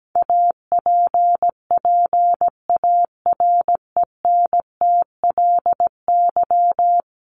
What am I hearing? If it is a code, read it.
Morse code, heard as a keyed tone, about 17 wpm, 700 Hz. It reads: APPARENTLY
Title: LY Suffix - Top 100 Words with Base Words 17wpm